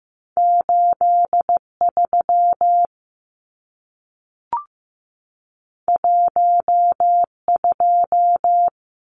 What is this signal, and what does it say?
Morse code, heard as a keyed tone, about 15 wpm, 700 Hz. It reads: 83 12
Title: Numbers - 2-Digit 15wpm